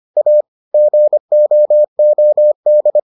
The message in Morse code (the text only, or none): A GOOD